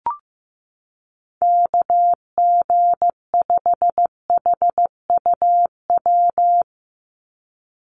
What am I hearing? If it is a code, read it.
Morse code, heard as a keyed tone, about 15 wpm, 700 Hz. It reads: KG5HUW
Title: USA Call Signs 15wpm